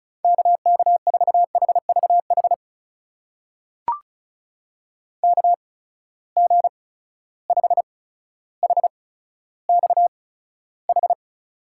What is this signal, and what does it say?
Morse code, heard as a keyed tone, about 36 wpm, spaced as for 15 wpm, 700 Hz. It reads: KK4HVH KG5HXH
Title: USA Call Signs - 6 Characters - 8x Character Spacing - 35wpm